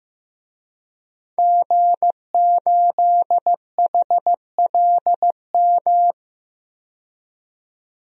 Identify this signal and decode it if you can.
Morse code, heard as a keyed tone, about 15 wpm, 700 Hz. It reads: G8HLM
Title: DXCC-100 Call Signs - 5 Characters - 15wpm